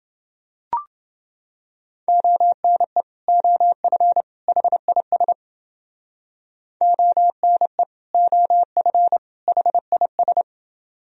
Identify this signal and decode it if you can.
Morse code, heard as a keyed tone, about 29 wpm, 700 Hz. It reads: ONE OF HIS ONE OF HIS